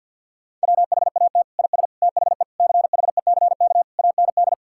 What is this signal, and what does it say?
Morse code, heard as a keyed tone, about 50 wpm, 700 Hz. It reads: WHAT IS THE CHECK AND
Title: Sentences from Top 1000 Words - Review Part 6 - 50wpm